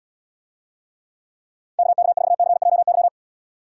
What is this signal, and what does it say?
Morse code, heard as a keyed tone, about 77 wpm, 700 Hz. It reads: KK4CJY